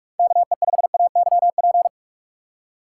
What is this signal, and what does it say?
Morse code, heard as a keyed tone, about 46 wpm, 700 Hz. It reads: KE5AYP
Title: USA Call Signs - 6 Characters - 2x Character Spacing - 45wpm